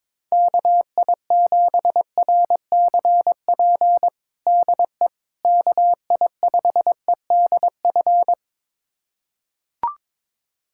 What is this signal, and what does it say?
Morse code, heard as a keyed tone, about 22 wpm, 700 Hz. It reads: KI7RCP DE KI5EDF
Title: USA Callsign DE Callsign 22wpm